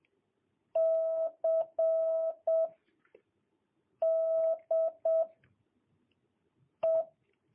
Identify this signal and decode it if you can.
Morse code, heard as a keyed tone, about 7 wpm, 1300 Hz. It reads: C D E
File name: jh8cbh / 2022cw